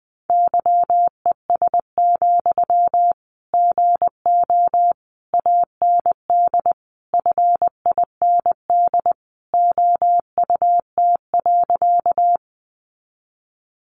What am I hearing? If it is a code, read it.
Morse code, heard as a keyed tone, about 20 wpm, 700 Hz. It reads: YES, GO AND FIND OUT.
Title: Sentences from Top 500 Words 20wpm